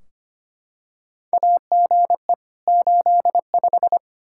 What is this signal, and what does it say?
Morse code, heard as a keyed tone, about 25 wpm, 700 Hz. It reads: AGE 85
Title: Practice QSO - Exchange 3 - More Info - 25wpm